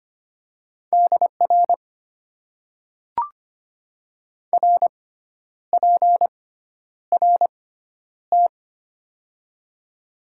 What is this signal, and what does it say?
Morse code, heard as a keyed tone, about 25 wpm, spaced as for 10 wpm, 700 Hz. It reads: DR RPRT
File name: Top 90 QSO Elements - 6x Character Spacing - 25wpm